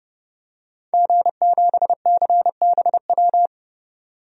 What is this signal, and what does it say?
Morse code, heard as a keyed tone, about 29 wpm, 700 Hz. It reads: G7CBW
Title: DXCC-100 Call Signs - 5 Characters - 30wpm